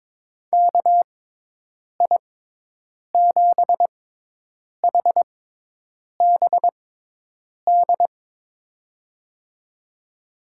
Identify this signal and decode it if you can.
Morse code, heard as a keyed tone, about 22 wpm, spaced as for 8 wpm, 700 Hz. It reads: KI7HBD
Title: USA Call Signs - 6 Characters - 6x Character Spacing - 22wpm